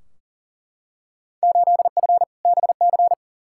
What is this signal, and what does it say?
Morse code, heard as a keyed tone, about 40 wpm, 700 Hz. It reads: 8F BC
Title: Field Day Exchanges Encore - 40wpm